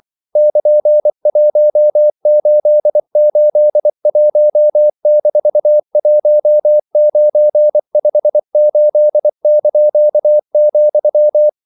Morse code, heard as a keyed tone, about 24 wpm, 600 Hz. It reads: <KN>1881-1958),
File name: CW / Practice / EnglishQuotations0424